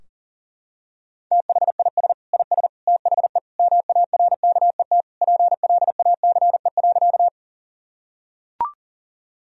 Morse code, heard as a keyed tone, about 40 wpm, 700 Hz. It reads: THIS IS THE MARKET PLACE.
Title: Sentences from Top 1000 Words - Review Part 3 - 40wpm